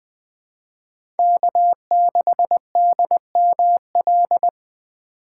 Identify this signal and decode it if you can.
Morse code, heard as a keyed tone, about 20 wpm, 700 Hz. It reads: K6DML
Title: USA Call Signs - 4x Character Spacing - 20wpm